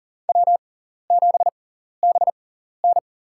Morse code, heard as a keyed tone, about 44 wpm, spaced as for 14 wpm, 700 Hz. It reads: W7BN
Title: USA Call Signs - 4 Characters - 6x Character Spacing - 40wpm